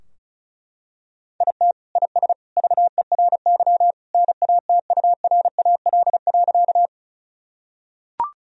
Morse code, heard as a keyed tone, about 36 wpm, 700 Hz. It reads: IT IS VERY NATURAL.
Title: Sentences from Top 1000 Words - Review Part 4 - 35wpm